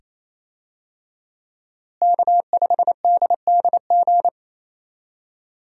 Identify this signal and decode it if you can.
Morse code, heard as a keyed tone, about 28 wpm, 700 Hz. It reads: K5DDG
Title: USA Call Signs - 5 Characters - Repeated 2x - 28wpm